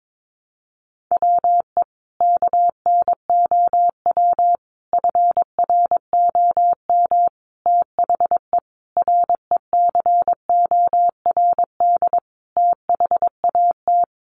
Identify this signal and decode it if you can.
Morse code, heard as a keyed tone, about 22 wpm, 700 Hz. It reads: WE KNOW FROM THE RECORD THAT
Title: Sentences from Top 500 Words 22wpm